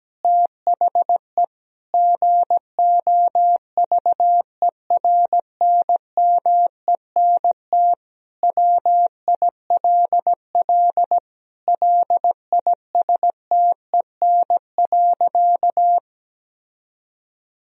Morse code, heard as a keyed tone, about 17 wpm, 700 Hz. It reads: THE GOVERNMENT WILL LISTEN.
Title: Sentences from Top 1000 Words - Review Part 3 - 17wpm